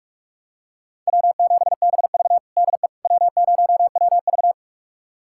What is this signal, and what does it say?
Morse code, heard as a keyed tone, about 46 wpm, 700 Hz. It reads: W7BV DE W0WV
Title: USA Callsign DE Callsign - 4 Characters - 45wpm